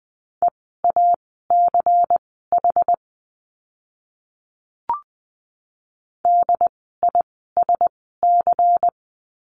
Morse code, heard as a keyed tone, about 20 wpm, spaced as for 15 wpm, 700 Hz. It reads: EACH DISC
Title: 094 Lesson 16f - Letter c - Words 20wpm